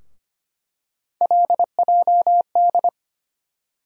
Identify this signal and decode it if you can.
Morse code, heard as a keyed tone, about 25 wpm, 700 Hz. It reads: LJD
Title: USA Call Sign Postfixes - 3 Letters 25wpm